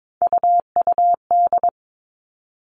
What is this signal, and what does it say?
Morse code, heard as a keyed tone, about 22 wpm, 700 Hz. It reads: UUD
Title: All Cut Numbers - 3-Digit 22wpm